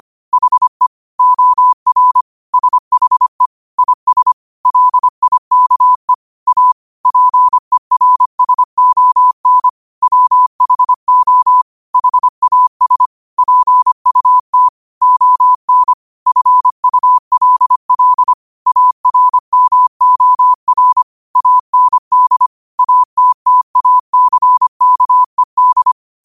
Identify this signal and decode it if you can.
Morse code, heard as a keyed tone, about 25 wpm, 1000 Hz. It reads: HE OR SHE IS LIKE A PERSON WHO HAS PUT ON FULL ARMOR AND ATTACKED